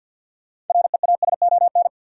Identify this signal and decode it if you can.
Morse code, heard as a keyed tone, about 47 wpm, 700 Hz. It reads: REASON
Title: Sets of 5 Words Encore - New Word Drill - 50wpm